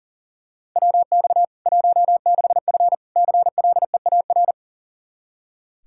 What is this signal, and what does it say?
Morse code, heard as a keyed tone, about 40 wpm, 700 Hz. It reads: WX 16F CLEAR
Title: Practice QSO - Exchange 2 - Info - 40wpm